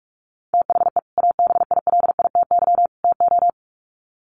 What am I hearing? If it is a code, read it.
Morse code, heard as a keyed tone, about 44 wpm, 700 Hz. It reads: THE ABILITY TO
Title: Sets of 3 Words 45wpm